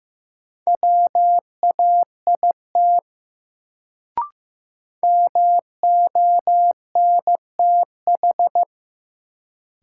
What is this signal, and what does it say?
Morse code, heard as a keyed tone, about 15 wpm, 700 Hz. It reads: WAIT MONTH
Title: Sets of 5 Words Encore - New Word Drill - 15wpm